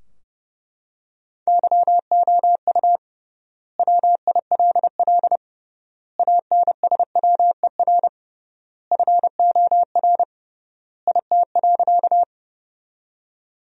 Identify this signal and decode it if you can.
Morse code, heard as a keyed tone, about 30 wpm, 700 Hz. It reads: YOU WILL ANSWER FOR IT.
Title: 3x Word-Spacing - Sentences from Top 300 Words - 30wpm